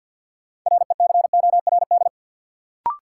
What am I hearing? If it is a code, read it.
Morse code, heard as a keyed tone, about 52 wpm, 700 Hz. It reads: RECORD
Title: Sentences from Top 500 Words - New Word Drill - 50wpm